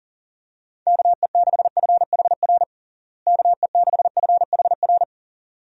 Morse code, heard as a keyed tone, about 40 wpm, 700 Hz. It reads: KE6FHR KE6FHR
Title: USA Call Signs - 6 Characters - Repeated 2x - 40wpm